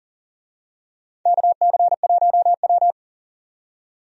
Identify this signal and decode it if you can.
Morse code, heard as a keyed tone, about 40 wpm, 700 Hz. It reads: KC1W
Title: USA Call Signs - 4 Characters - 40wpm